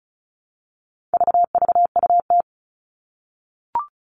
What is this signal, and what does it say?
Morse code, heard as a keyed tone, about 34 wpm, 700 Hz. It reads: VVUT
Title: All Cut Numbers - 4-Digit 35wpm